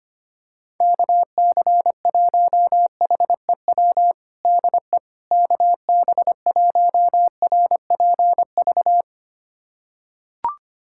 Morse code, heard as a keyed tone, about 25 wpm, 700 Hz. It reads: KC1HEW DE KB1RPV
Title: USA Callsign DE Callsign - 6 Characters - 25wpm